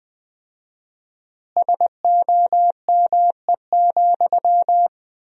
Morse code, heard as a keyed tone, about 20 wpm, 700 Hz. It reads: SOME,
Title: 4x Word-Spacing - Sentences from Top 200 Words - 20wpm